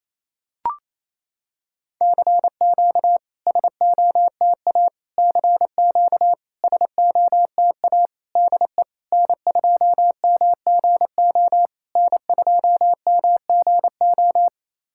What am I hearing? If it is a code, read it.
Morse code, heard as a keyed tone, about 28 wpm, 700 Hz. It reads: CQ SOTA CQ SOTA DE N2MGO N2MGO
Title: SOTA QSO Practice 28wpm